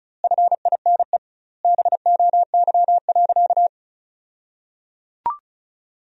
Morse code, heard as a keyed tone, about 34 wpm, 700 Hz. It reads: FINE BOY.
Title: 2x Word-Spacing - Sentences from Top 500 Words - 35wpm